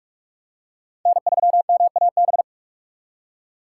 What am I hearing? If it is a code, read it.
Morse code, heard as a keyed tone, about 46 wpm, 700 Hz. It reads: N3MAB
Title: USA Call Signs - 5 Characters - 4x Character Spacing - 45wpm